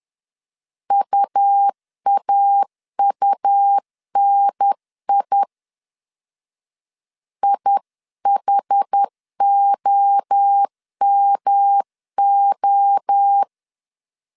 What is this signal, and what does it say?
Morse code, heard as a keyed tone, about 10 wpm, 800 Hz. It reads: UAUNI IHOMO
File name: 10wpm